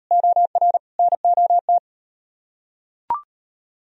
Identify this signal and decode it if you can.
Morse code, heard as a keyed tone, about 38 wpm, 700 Hz. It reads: OR NOT